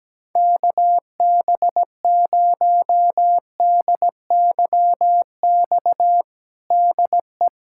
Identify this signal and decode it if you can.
Morse code, heard as a keyed tone, about 17 wpm, 700 Hz. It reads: KB0DYX DE
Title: USA Callsign DE Callsign - 6 Characters - 17wpm